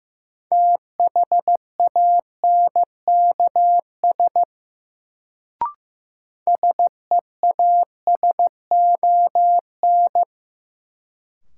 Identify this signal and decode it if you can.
Morse code, heard as a keyed tone, about 15 wpm, 700 Hz. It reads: THANKS SEASON
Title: Sets of 5 Words - New Word Drill - 15wpm